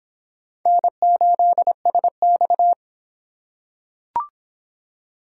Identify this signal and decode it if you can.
Morse code, heard as a keyed tone, about 25 wpm, 700 Hz. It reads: N8SX